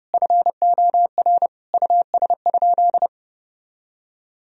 Morse code, heard as a keyed tone, about 32 wpm, 700 Hz. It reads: FOR US?
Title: Sentences from Top 1000 Words - Review Part 3 - 30wpm